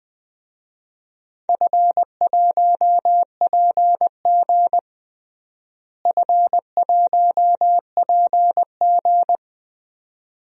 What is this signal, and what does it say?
Morse code, heard as a keyed tone, about 20 wpm, 700 Hz. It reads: F1PG F1PG
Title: DXCC-100 Call Signs - 4 Characters - Repeated 2x - 20wpm